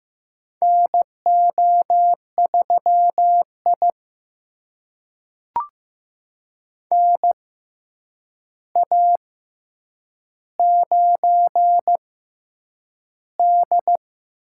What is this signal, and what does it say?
Morse code, heard as a keyed tone, about 15 wpm, spaced as for 8 wpm, 700 Hz. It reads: NO3I NA9D
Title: USA Call Signs - 4 Characters - 6x Character Spacing - 15wpm